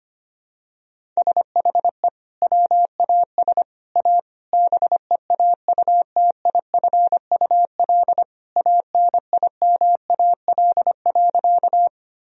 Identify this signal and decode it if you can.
Morse code, heard as a keyed tone, about 25 wpm, 700 Hz. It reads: SHE WAS A BEAUTIFUL ANIMAL.
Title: Sentences from Top 1000 Words - Review Part 1 - 25wpm